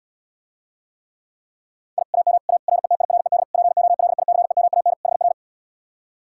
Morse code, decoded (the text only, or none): I AM A RETIRED POLICEMAN HR